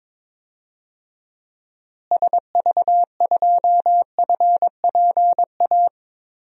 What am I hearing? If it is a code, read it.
Morse code, heard as a keyed tone, about 22 wpm, 700 Hz. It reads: SV2FPA
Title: DXCC-100 Call Signs - 6 Characters - Repeated 3x - 22wpm